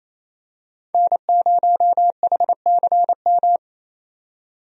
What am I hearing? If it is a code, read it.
Morse code, heard as a keyed tone, about 27 wpm, 700 Hz. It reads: N0HCM
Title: USA Call Signs - 5 Characters - 28wpm